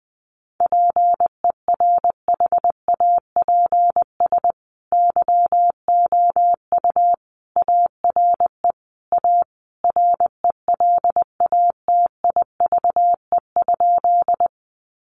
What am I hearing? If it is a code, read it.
Morse code, heard as a keyed tone, about 19 wpm, 700 Hz. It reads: PERHAPS YOU ARE A RELATIVE?
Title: Sentences from Top 1400 Words 20wpm